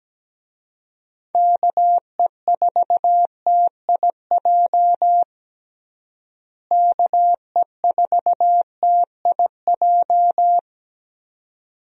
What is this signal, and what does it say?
Morse code, heard as a keyed tone, about 17 wpm, 700 Hz. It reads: KE4TIJ KE4TIJ
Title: USA Call Signs - 6 Characters - Repeated 2x - 17wpm